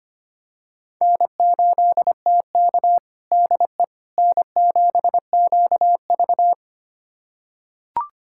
Morse code, heard as a keyed tone, about 25 wpm, 700 Hz. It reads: N8TK DE N7QV
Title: USA Callsign DE Callsign - 4 Characters - 25wpm